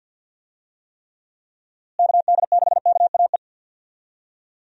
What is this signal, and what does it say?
Morse code, heard as a keyed tone, about 52 wpm, 700 Hz. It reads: KD6KAE
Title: USA Call Signs - 6 Characters - Repeated 2x - 50wpm